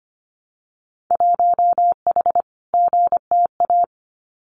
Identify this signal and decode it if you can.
Morse code, heard as a keyed tone, about 25 wpm, 700 Hz. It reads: 1H GTA
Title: Winter Field Day Exchanges Encore 25wpm